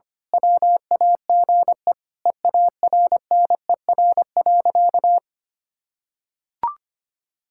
Morse code, heard as a keyed tone, about 25 wpm, 700 Hz. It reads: WAGE EARNER.
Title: Pangram Challenge v2 - 25wpm